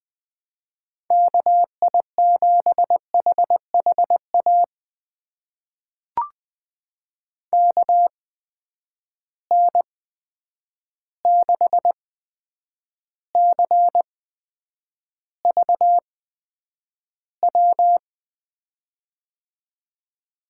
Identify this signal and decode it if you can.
Morse code, heard as a keyed tone, about 21 wpm, spaced as for 9 wpm, 700 Hz. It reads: KI7HHA KN6CVW
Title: USA Call Signs - 6 Characters - 8x Character Spacing - 20wpm